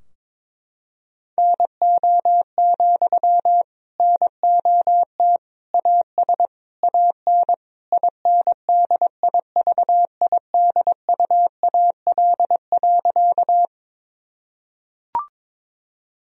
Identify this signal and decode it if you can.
Morse code, heard as a keyed tone, about 22 wpm, 700 Hz. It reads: NO, NOT AS AN INDIVIDUAL.
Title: Sentences from Top 1000 Words - Review Part 4 - 22wpm